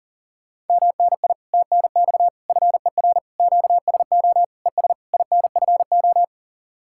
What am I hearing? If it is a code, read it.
Morse code, heard as a keyed tone, about 50 wpm, 700 Hz. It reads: MNI TNX FER QSO ES INFO